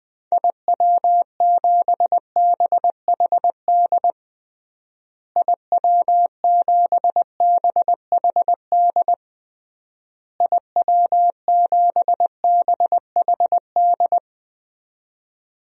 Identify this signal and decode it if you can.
Morse code, heard as a keyed tone, about 20 wpm, 700 Hz. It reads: IW7BHD IW7BHD IW7BHD
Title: DXCC-100 Call Signs - 6 Characters - Repeated 3x - 20wpm